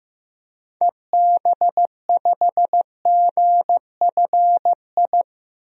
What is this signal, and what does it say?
Morse code, heard as a keyed tone, about 16 wpm, 700 Hz. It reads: EB5GFI
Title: DXCC-100 Call Signs - 6 Characters - Repeated 2x - 15wpm